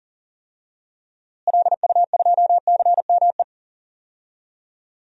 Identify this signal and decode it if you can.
Morse code, heard as a keyed tone, about 40 wpm, 700 Hz. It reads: LU2CME
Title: DXCC-100 Call Signs - 6 Characters - Repeated 3x - 40wpm